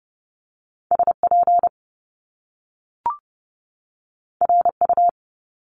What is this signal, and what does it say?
Morse code, heard as a keyed tone, about 30 wpm, 700 Hz. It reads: SP RU
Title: Top Two Letter Combinations in Words 30wpm